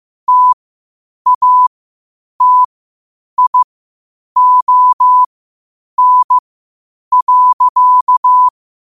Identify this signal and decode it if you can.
Morse code, heard as a keyed tone, about 15 wpm, spaced as for 8 wpm, 1000 Hz. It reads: TATION.
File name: qotdfiles / 8wpm